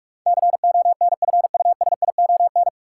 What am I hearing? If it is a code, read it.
Morse code, heard as a keyed tone, about 47 wpm, 700 Hz. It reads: CONFUSION
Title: Sentences from Top 1600 Words - New Word Drill - 45wpm